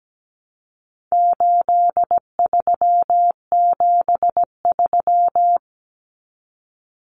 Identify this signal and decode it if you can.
Morse code, heard as a keyed tone, about 17 wpm, 700 Hz. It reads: 8373
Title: Numbers - 4-Digit 17wpm